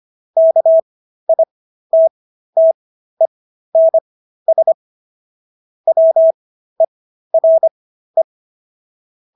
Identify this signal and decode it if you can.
Morse code, heard as a keyed tone, about 25 wpm, spaced as for 13 wpm, 650 Hz. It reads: KITTENS WERE